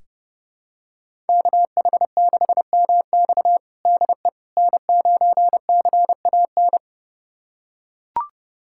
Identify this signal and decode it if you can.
Morse code, heard as a keyed tone, about 30 wpm, 700 Hz. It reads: KH6MX DE N9CAN
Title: USA Callsign DE Callsign - 5 Characters - 30wpm